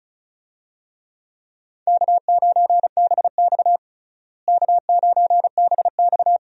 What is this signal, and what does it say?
Morse code, heard as a keyed tone, about 36 wpm, 700 Hz. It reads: K9BX K9BX
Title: USA Call Signs - 4 Characters - Repeated 2x - 35wpm